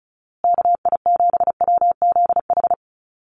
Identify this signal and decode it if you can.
Morse code, heard as a keyed tone, about 34 wpm, 700 Hz. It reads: KI7WZH
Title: USA Call Signs Repeated 2x 35wpm